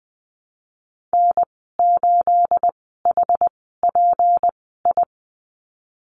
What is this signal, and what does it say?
Morse code, heard as a keyed tone, about 20 wpm, spaced as for 14 wpm, 700 Hz. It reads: N8HPI
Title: 219 Lesson 35g - Number 0 - Call Signs 20wpm